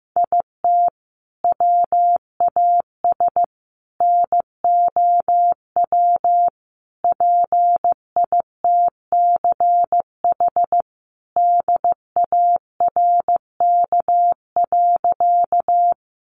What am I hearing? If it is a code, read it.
Morse code, heard as a keyed tone, about 15 wpm, 700 Hz. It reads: IT WAS NOW PITCH DARK.
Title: Sentences from Top 1300 Words 15wpm